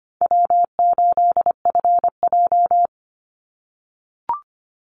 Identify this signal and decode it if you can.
Morse code, heard as a keyed tone, about 25 wpm, 700 Hz. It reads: W8FJ
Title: Easy Call Signs - 25wpm